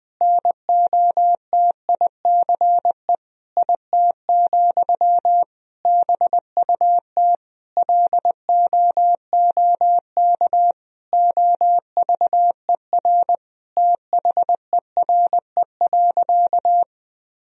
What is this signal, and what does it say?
Morse code, heard as a keyed tone, about 20 wpm, 700 Hz. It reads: NOTICE IT, BUT LOOK OVER THERE.
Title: Sentences from Top 1000 Words - Review Part 2 - 20wpm